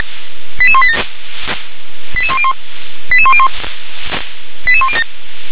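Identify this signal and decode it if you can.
Five-tone selective call (ZVEI), heard as quick runs of five tones, tones of about 70 ms. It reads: 88187 88101 88101 88107
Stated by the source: Five-tone ZVEI selective call received on SDR.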